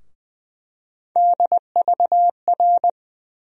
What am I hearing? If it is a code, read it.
Morse code, heard as a keyed tone, about 20 wpm, 700 Hz. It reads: DVR
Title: USA Call Sign Postfixes - 3 Letters 20wpm